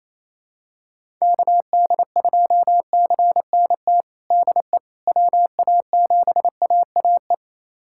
Morse code, heard as a keyed tone, about 28 wpm, 700 Hz. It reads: KD2CNT DE WA7AAE
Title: USA Callsign DE Callsign - 6 Characters - 28wpm